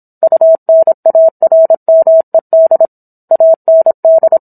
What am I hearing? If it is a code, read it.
Morse code, heard as a keyed tone, about 27 wpm, 650 Hz. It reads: UNARMED AND
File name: Book One / 17-Chapter 17 - 26 WPM